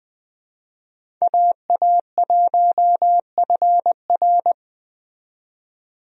Morse code, heard as a keyed tone, about 20 wpm, 700 Hz. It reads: AA1FR
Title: USA Call Signs - 5 Characters - 20wpm